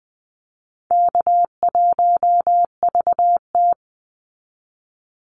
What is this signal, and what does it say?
Morse code, heard as a keyed tone, about 20 wpm, 700 Hz. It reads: K1VT
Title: Easy Call Signs Repeated 2x - 20wpm